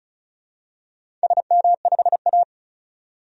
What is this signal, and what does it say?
Morse code, heard as a keyed tone, about 36 wpm, 700 Hz. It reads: SM5A
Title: DXCC-100 Call Signs - 4 Characters - 35wpm